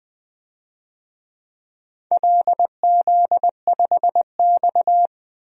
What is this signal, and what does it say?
Morse code, heard as a keyed tone, about 20 wpm, 700 Hz. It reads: LZ5X
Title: DXCC-100 Call Signs - 4 Characters - 20wpm